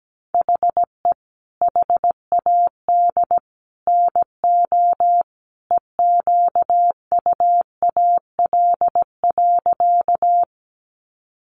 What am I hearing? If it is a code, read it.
Morse code, heard as a keyed tone, about 17 wpm, 700 Hz. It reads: HE HAD NO EQUAL.
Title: Sentences from Top 1400 Words 17wpm